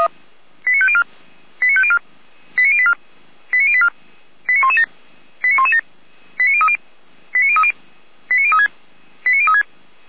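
Five-tone selective call (ZVEI), heard as quick runs of five tones, tones of about 70 ms. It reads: 89504 89504 89084 89084 89117 89117 89030 89030 89036 89036
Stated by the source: ZVEI five-tone selective call received on SDR, code 89115.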